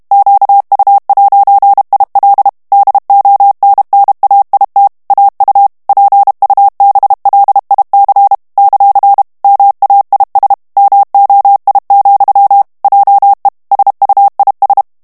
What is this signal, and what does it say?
Morse code, heard as a keyed tone, about 32 wpm, 800 Hz. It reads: QU'IL DONNAIT AU PUBLIC ; MAIS MOI, JE SUIS
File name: FRANCAIS TEXTE CLAIR 2 VITESSE 30 MOTS MINUTES